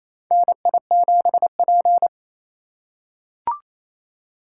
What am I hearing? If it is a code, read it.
Morse code, heard as a keyed tone, about 28 wpm, 700 Hz. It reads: NI7P
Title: USA Call Signs - 4 Characters - 3x Character Spacing - 28wpm